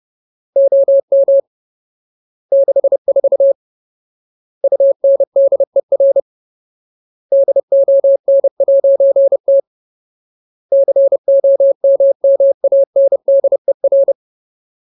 Morse code, heard as a keyed tone, about 30 wpm, 550 Hz. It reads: OM 64 UNDER DON'T COMMANDER
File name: Ultimate Word List / 30WPM Ultimate